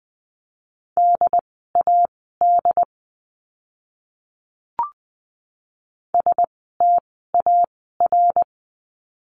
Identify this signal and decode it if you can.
Morse code, heard as a keyed tone, about 20 wpm, spaced as for 15 wpm, 700 Hz. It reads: DAD STAR
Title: 068 Lesson 12f - Letter d - Words 20wpm